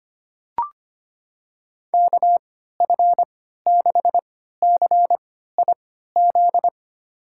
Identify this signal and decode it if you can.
Morse code, heard as a keyed tone, about 25 wpm, spaced as for 14 wpm, 700 Hz. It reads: KF6CIZ
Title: USA Call Signs - 6 Characters - 3x Character Spacing - 25wpm